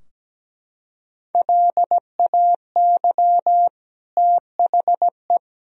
Morse code, heard as a keyed tone, about 17 wpm, 700 Hz. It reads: LAY THE
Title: Sentences from Top 1000 Words - Review Part 4 - 17wpm